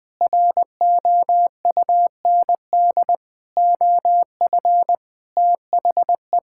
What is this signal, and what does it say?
Morse code, heard as a keyed tone, about 20 wpm, 700 Hz. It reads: ROUND OF THE
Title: Sentences with 7-Word Phrases - Spoken Beginning - 20wpm